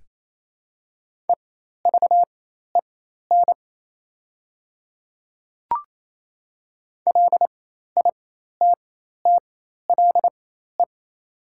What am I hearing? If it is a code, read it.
Morse code, heard as a keyed tone, about 28 wpm, spaced as for 13 wpm, 700 Hz. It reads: EVEN LITTLE
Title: 4x Character Spacing - Top 200 Words - 28wpm